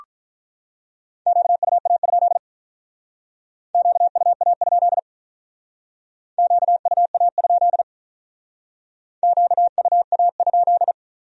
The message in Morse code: QUA? QUA? QUA? QUA?

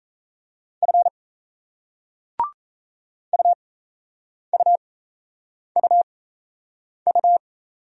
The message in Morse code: F UUUU

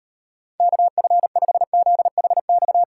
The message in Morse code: KF5ZHX

